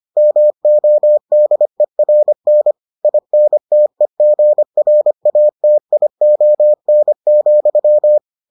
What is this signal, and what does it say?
Morse code, heard as a keyed tone, about 25 wpm, 600 Hz. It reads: MODERN INTEGRATION,